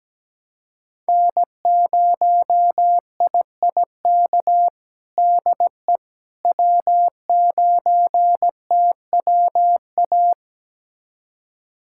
Morse code, heard as a keyed tone, about 17 wpm, 700 Hz. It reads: N0IIK DE W9TWA
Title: USA Callsign DE Callsign - 5 Characters - 17wpm